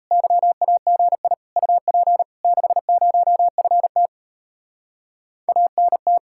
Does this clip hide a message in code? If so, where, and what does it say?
Yes: Morse code from 0.1 to 6.2 s, heard as a keyed tone, about 38 wpm, 700 Hz. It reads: YAGI UP 60FT ANT